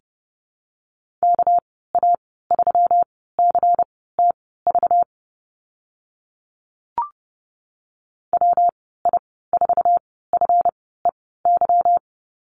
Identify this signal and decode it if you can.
Morse code, heard as a keyed tone, about 30 wpm, spaced as for 17 wpm, 700 Hz. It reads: KA3CTV WI4FEY
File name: Lesson 37g - Letter x - Call Signs 30wpm